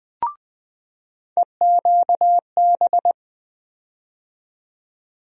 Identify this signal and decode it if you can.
Morse code, heard as a keyed tone, about 20 wpm, 700 Hz. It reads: EQB
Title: USA Call Sign Postfixes - 3 Letters 20wpm